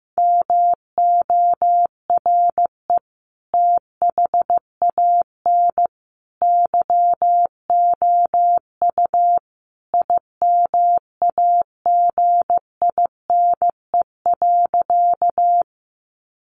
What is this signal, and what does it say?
Morse code, heard as a keyed tone, about 15 wpm, 700 Hz. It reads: MORE THAN YOU IMAGINE.
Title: Sentences from Top 1400 Words 15wpm